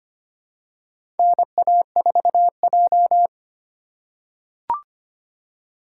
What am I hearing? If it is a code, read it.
Morse code, heard as a keyed tone, about 25 wpm, 700 Hz. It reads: NA4J
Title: USA Call Signs - 4 Characters - 4x Character Spacing - 25wpm